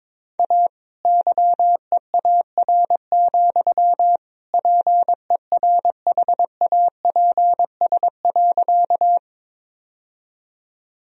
Morse code, heard as a keyed tone, about 22 wpm, 700 Hz. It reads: A YEAR, PERHAPS.
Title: Sentences from Top 1000 Words - Review Part 1 - 22wpm